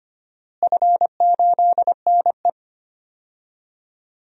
Morse code, heard as a keyed tone, about 25 wpm, 700 Hz. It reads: F8NE
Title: DXCC-100 Call Signs - 4 Characters - Repeated 2x - 25wpm